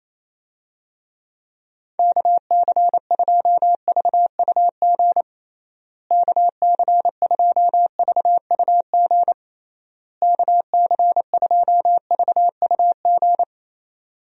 Morse code, heard as a keyed tone, about 28 wpm, 700 Hz. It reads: KC2VUG KC2VUG KC2VUG
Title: USA Call Signs - 6 Characters - Repeated 3x - 28wpm